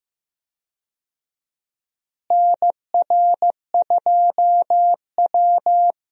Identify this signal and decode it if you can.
Morse code, heard as a keyed tone, about 15 wpm, 700 Hz. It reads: NR2W
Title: USA Call Signs - 4 Characters - Repeated 3x - 15wpm